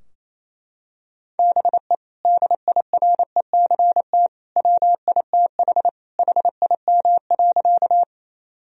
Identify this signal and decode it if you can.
Morse code, heard as a keyed tone, about 27 wpm, 700 Hz. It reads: BE DIRECT WITH HIM.
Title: Sentences from Top 1000 Words - Review Part 2 - 28wpm